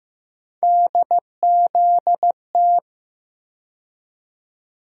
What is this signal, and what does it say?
Morse code, heard as a keyed tone, about 15 wpm, 700 Hz. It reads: DZT
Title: USA Call Sign Postfixes - 3 Letters 15wpm